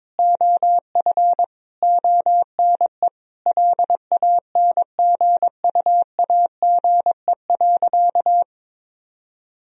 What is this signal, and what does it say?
Morse code, heard as a keyed tone, about 22 wpm, 700 Hz. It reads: OF ONE LANGUAGE.
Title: Sentences from Top 1000 Words - Review Part 3 - 22wpm